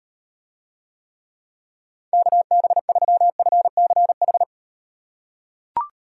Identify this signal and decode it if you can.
Morse code, heard as a keyed tone, about 38 wpm, 700 Hz. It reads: KB3FCH